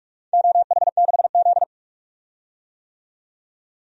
Morse code, heard as a keyed tone, about 47 wpm, 700 Hz. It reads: OH6Z
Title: DXCC-100 Call Signs - 4 Characters - 45wpm